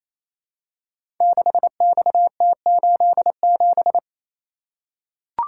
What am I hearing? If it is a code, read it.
Morse code, heard as a keyed tone, about 28 wpm, 700 Hz. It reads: 6XT87
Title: Code Groups - 5 Letter:Numbers - 28wpm